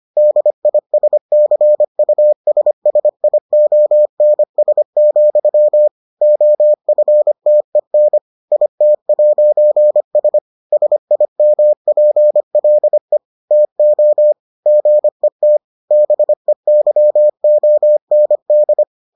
Morse code, heard as a keyed tone, about 25 wpm, 600 Hz. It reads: DISCUSSIONS, OFTEN IT'S SIMPLE TO GET BEYOND